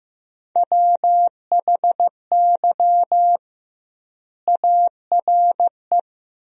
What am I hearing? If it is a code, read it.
Morse code, heard as a keyed tone, about 16 wpm, 700 Hz. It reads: WHY ARE
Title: 2x Word-Spacing - Sentences from Top 200 Words - 15wpm